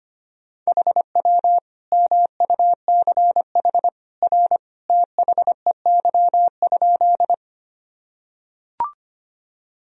Morse code, heard as a keyed tone, about 25 wpm, 700 Hz. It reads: HW MUCH R THEY?
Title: Sentences from Top 200 Words and Abbreviations 25wpm